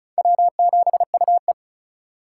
Morse code, heard as a keyed tone, about 45 wpm, 700 Hz. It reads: W7UE